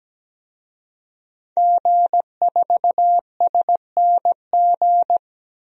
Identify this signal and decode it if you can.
Morse code, heard as a keyed tone, about 17 wpm, 700 Hz. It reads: G4SNG
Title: DXCC-100 Call Signs - 5 Characters - 17wpm